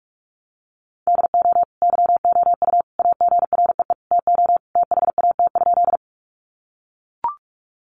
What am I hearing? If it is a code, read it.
Morse code, heard as a keyed tone, about 44 wpm, 700 Hz. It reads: DO YOU AGREE TO THAT?
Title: Sentences from Top 1100 Words 45wpm